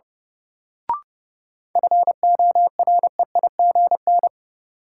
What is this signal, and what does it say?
Morse code, heard as a keyed tone, about 30 wpm, 700 Hz. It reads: FOREIGN
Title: Sentences from Top 1400 Words - New Word Drill - 30wpm